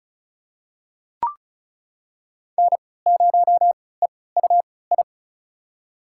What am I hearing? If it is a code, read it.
Morse code, heard as a keyed tone, about 36 wpm, spaced as for 20 wpm, 700 Hz. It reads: N0EUI
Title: USA Call Signs - 3x Character Spacing - 35wpm